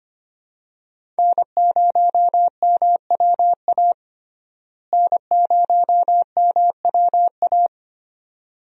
Morse code, heard as a keyed tone, about 25 wpm, 700 Hz. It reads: N0MWA N0MWA
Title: USA Call Signs - 5 Characters - Repeated 2x - 25wpm